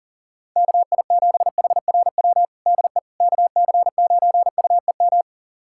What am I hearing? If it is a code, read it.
Morse code, heard as a keyed tone, about 40 wpm, 700 Hz. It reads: KI7HRW DE KC9UEM
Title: USA Callsign DE Callsign - 6 Characters - 40wpm